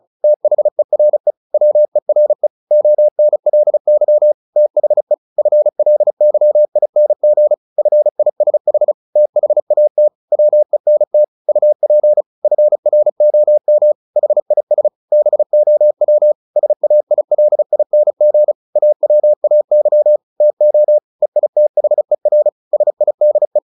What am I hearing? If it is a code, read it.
Morse code, heard as a keyed tone, about 34 wpm, 600 Hz. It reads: THERE WERE ONLY THE FLYING FISH THAT WENT UP FROM HIS BOW SAILING AWAY TO EITHER SIDE
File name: the old man and the sea 2 35wpm